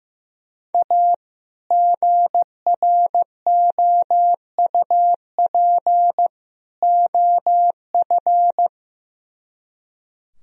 Morse code, heard as a keyed tone, about 15 wpm, 700 Hz. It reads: A GROUP OF
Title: Speed-Racing - Sets of 3 Words 15wpm